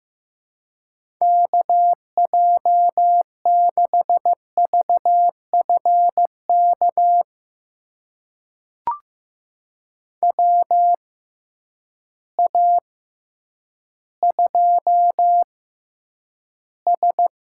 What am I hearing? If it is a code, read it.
Morse code, heard as a keyed tone, about 15 wpm, spaced as for 9 wpm, 700 Hz. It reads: KJ6VFK WA2S
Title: USA Call Signs - 6x Character Spacing - 15wpm